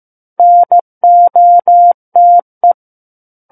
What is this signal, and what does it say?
Morse code, heard as a keyed tone, about 15 wpm, 700 Hz. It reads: NOTE
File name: Chapter 6 - 15 WPM